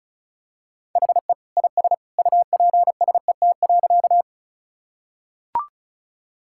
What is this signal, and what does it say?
Morse code, heard as a keyed tone, about 34 wpm, 700 Hz. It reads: HE IS UPSET.
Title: Sentences from Top 1000 Words - Review Part 5 - 35wpm